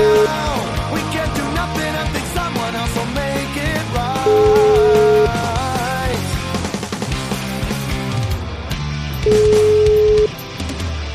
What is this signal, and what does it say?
Ringback tone (Europe, 425 Hz), with noise.